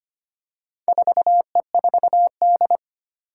Morse code, heard as a keyed tone, about 25 wpm, 700 Hz. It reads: 4E4D
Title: Four Challenging Letter-Number Combinations 25wpm